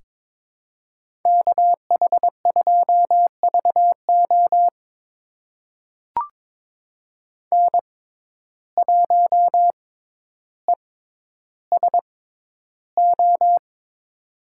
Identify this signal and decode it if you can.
Morse code, heard as a keyed tone, about 22 wpm, spaced as for 11 wpm, 700 Hz. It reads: KH2VO N1ESO
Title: USA Call Signs - 5 Characters - 6x Character Spacing - 22wpm